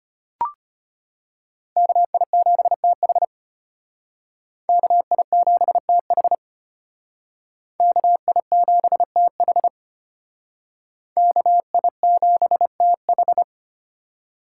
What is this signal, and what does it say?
Morse code, heard as a keyed tone, about 33 wpm, 700 Hz. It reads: KI7TH KI7TH KI7TH KI7TH